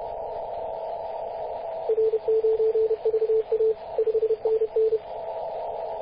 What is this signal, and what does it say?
Morse code, heard as a keyed tone, about 30 wpm, 440 Hz, with radio noise. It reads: R9VA 5NN
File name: posts / 2016-mar-19-0241z-vk9ck-21-032mhz-cw